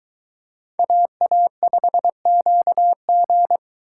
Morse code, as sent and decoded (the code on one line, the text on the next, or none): .- .- ..... --.- --.
AA5QG